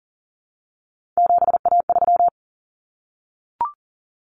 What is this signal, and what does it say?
Morse code, heard as a keyed tone, about 40 wpm, 700 Hz. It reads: 7A3
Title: Common Cut Numbers - 3-Digit 40wpm